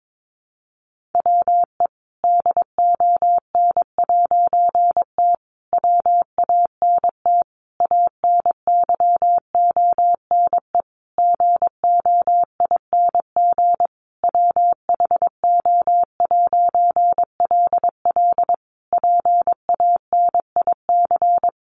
Morse code, heard as a keyed tone, about 22 wpm, 700 Hz. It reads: WE DON'T WANT ANYONE GOING WHO'LL PANIC
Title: A Sound Of Thunder 22wpm